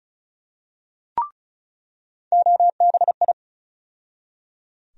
Morse code, heard as a keyed tone, about 34 wpm, 700 Hz. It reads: OBI
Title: USA Call Sign Postfixes - 3 Letters 35wpm